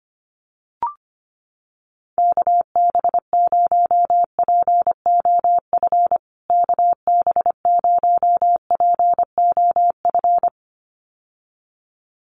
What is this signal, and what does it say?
Morse code, heard as a keyed tone, about 25 wpm, 700 Hz. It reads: KB0POF KB0POF
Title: USA Call Signs Repeated 2x 25wpm